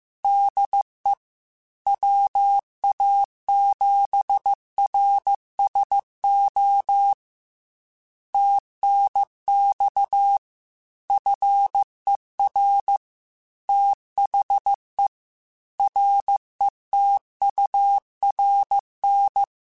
Morse code, heard as a keyed tone, about 15 wpm, 800 Hz. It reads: DE WA7RSO TNX FER THE RETURN
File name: QSOs / QSOs-mp3s / kc7sac@13